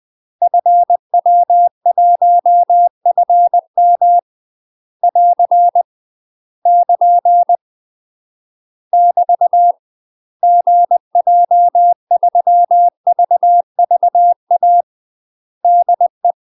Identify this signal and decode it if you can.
Morse code, heard as a keyed tone, about 20 wpm, 700 Hz. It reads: FW1FM <AR> <KN> <BT> GJ3VVA DE